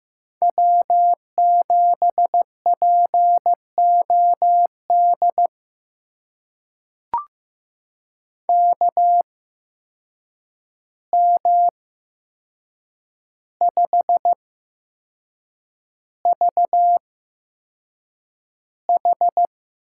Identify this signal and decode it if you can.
Morse code, heard as a keyed tone, about 15 wpm, spaced as for 6 wpm, 700 Hz. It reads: W7POD KM5VH
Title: USA Call Signs - 5 Characters - 8x Character Spacing - 15wpm